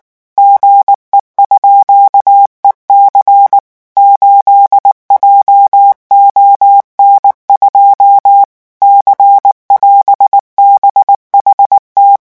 Morse code, heard as a keyed tone, about 19 wpm, 800 Hz. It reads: GE_EC 8JON2 C<AS>BHT
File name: morse mp3 / revision17-vitesse 18 mots